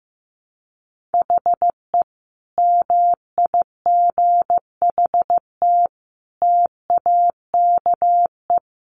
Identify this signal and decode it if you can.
Morse code, heard as a keyed tone, about 15 wpm, 700 Hz. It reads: HE MIGHT TAKE
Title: Sentences from Top 1000 Words 15wpm